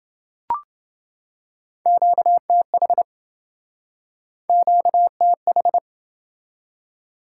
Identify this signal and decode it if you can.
Morse code, heard as a keyed tone, about 28 wpm, 700 Hz. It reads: QTH QTH